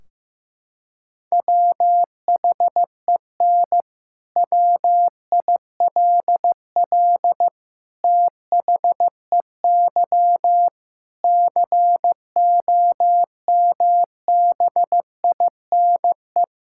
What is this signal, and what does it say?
Morse code, heard as a keyed tone, about 15 wpm, 700 Hz. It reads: WHEN WILL THEY COMBINE
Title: Sentences from Top 1600 Words 15wpm